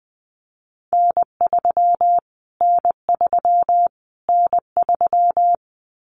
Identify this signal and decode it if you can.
Morse code, heard as a keyed tone, about 20 wpm, 700 Hz. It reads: N3 N3 N3
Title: Callsign Prefixes for Top 25 DXCC - Repeated 3x - 20wpm